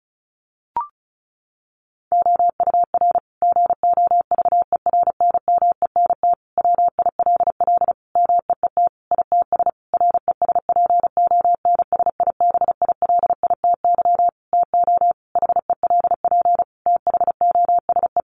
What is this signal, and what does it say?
Morse code, heard as a keyed tone, about 34 wpm, 700 Hz. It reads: OUR GOVERNMENT WILL MEET ITS RESPONSIBILITY TO HELP THOSE
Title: Sentences from Top 1200 Words 35wpm